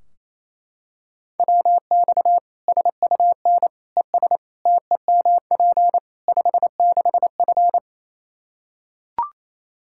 Morse code, heard as a keyed tone, about 28 wpm, 700 Hz. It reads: WX SUN ES TEMP 56F
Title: Practice QSO - Exchange 2 - Info - 28wpm